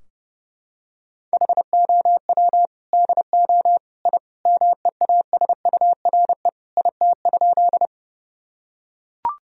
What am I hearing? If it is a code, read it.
Morse code, heard as a keyed tone, about 30 wpm, 700 Hz. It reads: HOW DO I MEASURE IT?
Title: Sentences from Top 1000 Words - Review Part 2 - 30wpm